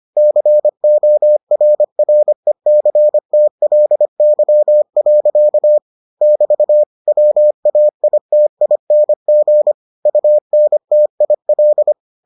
Morse code, heard as a keyed tone, about 25 wpm, 600 Hz. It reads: CORRECTLY. <BT> WAITING UNTIL